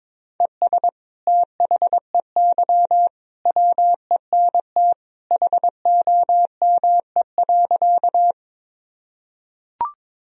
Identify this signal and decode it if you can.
Morse code, heard as a keyed tone, about 22 wpm, 700 Hz. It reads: ES THEY WENT HOME.
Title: Sentences from Top 200 Words and Abbreviations 22wpm